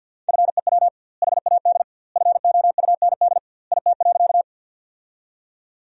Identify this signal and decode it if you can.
Morse code, heard as a keyed tone, about 52 wpm, 700 Hz. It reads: FEW HAD FOUND IT.
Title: 2x Word-Spacing - Sentences from Top 300 Words - 50wpm